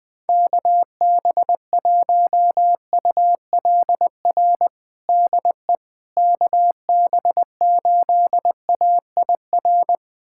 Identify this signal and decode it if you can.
Morse code, heard as a keyed tone, about 21 wpm, 700 Hz. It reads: KB1ULR DE KB8AIR
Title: USA Callsign DE Callsign 20wpm